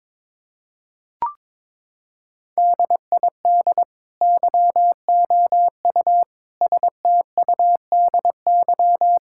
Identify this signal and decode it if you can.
Morse code, heard as a keyed tone, about 22 wpm, 700 Hz. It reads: DID YOU STUDY
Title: Sentences from Top 1000 Words - Review Part 6 - 22wpm